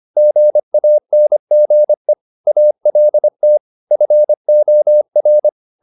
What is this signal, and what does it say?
Morse code, heard as a keyed tone, about 25 wpm, 600 Hz. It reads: GANGE ALT FOR
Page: Ferie 25 wpm